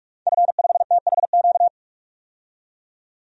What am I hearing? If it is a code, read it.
Morse code, heard as a keyed tone, about 41 wpm, 700 Hz. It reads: F5THQ
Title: DXCC-100 Call Signs - 5 Characters - 45wpm